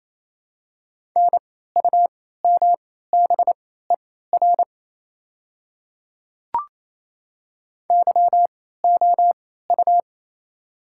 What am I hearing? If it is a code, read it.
Morse code, heard as a keyed tone, about 28 wpm, spaced as for 16 wpm, 700 Hz. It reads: NUMBER YOU
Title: 3x Character Spacing - Top 100 Words - 28wpm